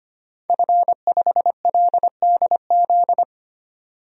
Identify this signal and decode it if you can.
Morse code, heard as a keyed tone, about 25 wpm, 700 Hz. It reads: F5LDZ
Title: DXCC-100 Call Signs - 5 Characters - 25wpm